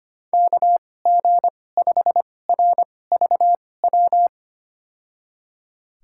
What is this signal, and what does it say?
Morse code, heard as a keyed tone, about 25 wpm, spaced as for 18 wpm, 700 Hz. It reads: KG5RVW
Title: USA Call Signs - 2x Character Spacing - 25wpm